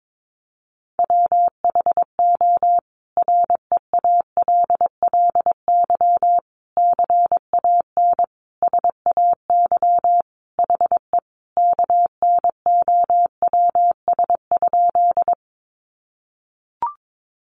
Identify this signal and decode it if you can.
Morse code, heard as a keyed tone, about 22 wpm, 700 Hz. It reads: WHO REALLY CAN SAY HE KNOWS?
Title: A Sound Of Thunder 22wpm